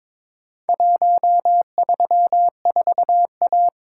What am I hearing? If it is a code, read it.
Morse code, heard as a keyed tone, about 23 wpm, 700 Hz. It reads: 134A
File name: practice / 22wpm / Four Letter-Number Combinations - 22wpm